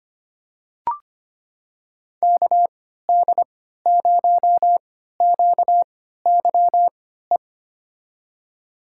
Morse code, heard as a keyed tone, about 25 wpm, spaced as for 14 wpm, 700 Hz. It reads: KD0QYE
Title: USA Call Signs - 6 Characters - 3x Character Spacing - 25wpm